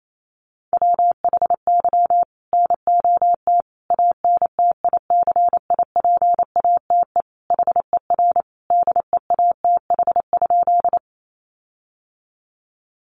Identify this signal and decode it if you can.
Morse code, heard as a keyed tone, about 28 wpm, 700 Hz. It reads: WHY NOT ANTICIPATE HER DEATH?
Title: Sentences from Top 1500 Words 28wpm